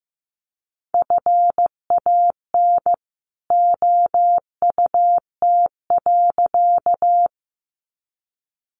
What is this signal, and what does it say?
Morse code, heard as a keyed tone, about 15 wpm, 700 Hz. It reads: FAN OUT.
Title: Sentences from Top 1300 Words 15wpm